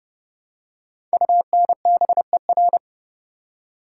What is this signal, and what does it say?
Morse code, heard as a keyed tone, about 30 wpm, 690 Hz. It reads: UNBER
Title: Code Groups - 5 Letters - 30wpm